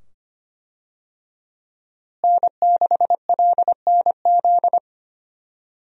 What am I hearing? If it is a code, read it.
Morse code, heard as a keyed tone, about 25 wpm, 700 Hz. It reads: N6LNZ
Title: USA Call Signs - 5 Characters - Repeated 2x - 25wpm